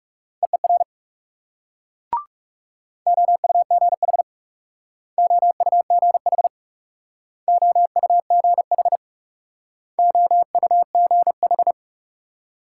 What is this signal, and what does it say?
Morse code, heard as a keyed tone, about 42 wpm, 700 Hz. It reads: EER OUGH OUGH OUGH OUGH